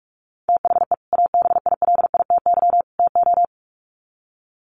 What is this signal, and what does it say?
Morse code, heard as a keyed tone, about 44 wpm, 700 Hz. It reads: THE ABILITY TO
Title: Sets of 3 Words 45wpm